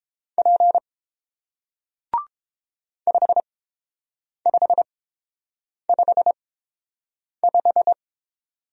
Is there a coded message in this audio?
Yes: Morse code from 0.4 to 7.9 s, heard as a keyed tone, about 30 wpm, spaced as for 7 wpm, 700 Hz. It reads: P 5555